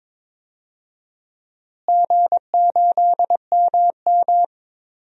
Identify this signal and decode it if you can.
Morse code, heard as a keyed tone, about 22 wpm, 700 Hz. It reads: G8MM
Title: DXCC-100 Call Signs - 4 Characters - Repeated 3x - 22wpm